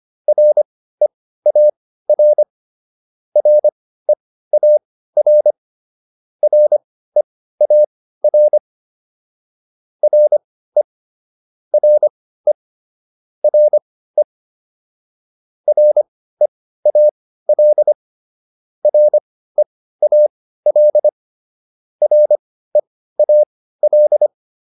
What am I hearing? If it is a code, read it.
Morse code, heard as a keyed tone, about 25 wpm, spaced as for 15 wpm, 600 Hz. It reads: REAR REAR REAR RE RE RE REAL REAL REAL
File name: RE4 15